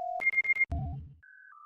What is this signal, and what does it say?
There is no code.